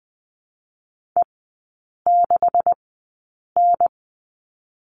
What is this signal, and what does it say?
Morse code, heard as a keyed tone, about 20 wpm, spaced as for 8 wpm, 700 Hz. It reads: E6N
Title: 127 Lesson 21e - Number 6 - Three Characters 20wpm